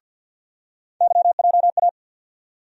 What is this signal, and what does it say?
Morse code, heard as a keyed tone, about 51 wpm, 700 Hz. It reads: YJA
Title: USA Call Sign Postfixes - 3 Letters 50wpm